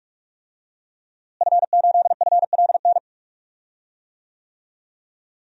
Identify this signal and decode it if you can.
Morse code, heard as a keyed tone, about 48 wpm, 700 Hz. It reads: F8FLN
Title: DXCC-100 Call Signs - 5 Characters - Repeated 2x - 45wpm